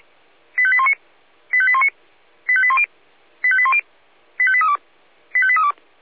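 Five-tone selective call (ZVEI), heard as quick runs of five tones, tones of about 70 ms. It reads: 85819 85819 85810 85810 85832 85832
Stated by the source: Received on SDR, ZVEI five-tone selective call.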